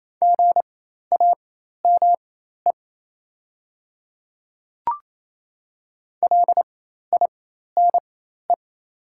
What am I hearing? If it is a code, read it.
Morse code, heard as a keyed tone, about 28 wpm, spaced as for 13 wpm, 690 Hz. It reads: GAME LINE
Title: 4x Character Spacing - Top 500 Words - 28wpm